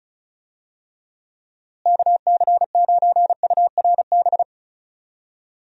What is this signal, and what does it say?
Morse code, heard as a keyed tone, about 36 wpm, 700 Hz. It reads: KC9URB
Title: USA Call Signs - 6 Characters - Repeated 2x - 35wpm